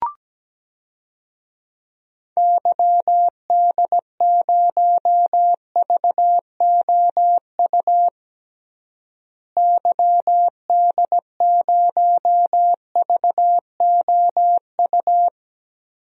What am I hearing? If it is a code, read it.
Morse code, heard as a keyed tone, about 17 wpm, 700 Hz. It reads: YD0VOU YD0VOU
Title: DXCC-100 Call Signs - 6 Characters - Repeated 3x - 17wpm